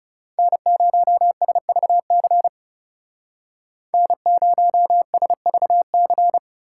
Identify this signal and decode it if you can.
Morse code, heard as a keyed tone, about 36 wpm, 700 Hz. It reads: N0SVC N0SVC
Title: Speed Racing - USA 1x3 Call Signs 30wpm